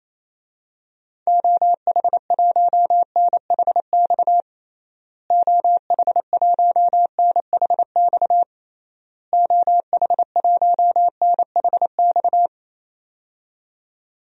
Morse code, heard as a keyed tone, about 28 wpm, 700 Hz. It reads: OH1NHX OH1NHX OH1NHX
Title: DXCC-100 Call Signs - 6 Characters - Repeated 3x - 28wpm